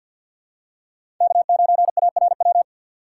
Morse code, heard as a keyed tone, about 53 wpm, 700 Hz. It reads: K9ARW
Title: USA Call Signs - 4x Character Spacing - 50wpm